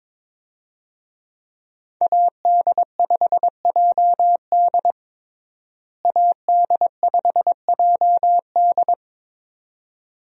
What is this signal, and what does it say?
Morse code, heard as a keyed tone, about 22 wpm, 700 Hz. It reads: AD5JD AD5JD
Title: USA Call Signs - 5 Characters - Repeated 2x - 22wpm